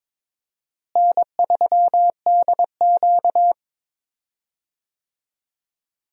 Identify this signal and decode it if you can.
Morse code, heard as a keyed tone, about 22 wpm, 700 Hz. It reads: N3DQ
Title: USA Call Signs - 4 Characters - Repeated 2x - 22wpm